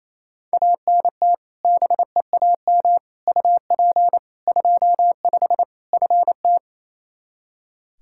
Morse code, heard as a keyed tone, about 27 wpm, 700 Hz. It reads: ANT BEAM UP 25 FT
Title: K1USN SST QSO Practice 28wpm